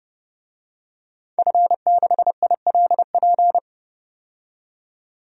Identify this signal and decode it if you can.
Morse code, heard as a keyed tone, about 32 wpm, 700 Hz. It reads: F6ILP
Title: DXCC-100 Call Signs - 5 Characters - 30wpm